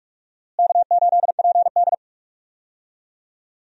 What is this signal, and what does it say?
Morse code, heard as a keyed tone, about 46 wpm, 700 Hz. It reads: K8PD
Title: USA Call Signs - 4 Characters - Repeated 3x - 45wpm